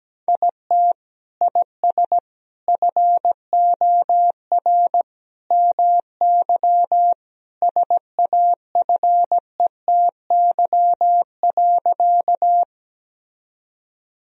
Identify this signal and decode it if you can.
Morse code, heard as a keyed tone, about 17 wpm, 700 Hz. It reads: IT IS FOR MY SAFETY.
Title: Sentences from Top 1000 Words - Review Part 4 - 17wpm